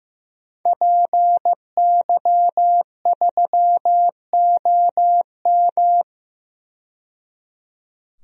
Morse code, heard as a keyed tone, about 15 wpm, 700 Hz. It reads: PY3OM
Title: DXCC-100 Call Signs - 5 Characters - 15wpm